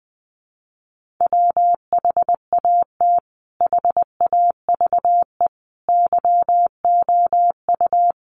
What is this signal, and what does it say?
Morse code, heard as a keyed tone, about 20 wpm, 700 Hz. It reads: WHAT HAVE YOU
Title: Sentences from Top 1400 Words 20wpm